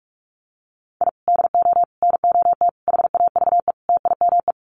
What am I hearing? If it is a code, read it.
Morse code, heard as a keyed tone, about 44 wpm, 700 Hz. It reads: I DO NOT HAVE TIME
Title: Sets of 5 Words Encore 45wpm